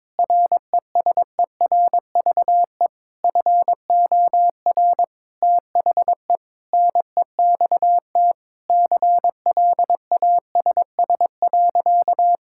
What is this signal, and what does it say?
Morse code, heard as a keyed tone, about 21 wpm, 700 Hz. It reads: RESERVE FOR THE NEXT CLASS.
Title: Sentences from Top 1000 Words - Review Part 6 - 22wpm